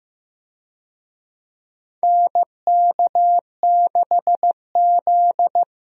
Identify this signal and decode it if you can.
Morse code, heard as a keyed tone, about 15 wpm, 700 Hz. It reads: NK6Z
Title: USA Call Signs - 4 Characters - Repeated 3x - 15wpm